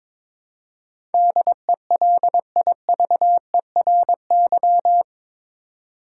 Morse code, heard as a keyed tone, about 22 wpm, 700 Hz. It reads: DELIVERY
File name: practice / 22wpm / Eight Letter Words - 22wpm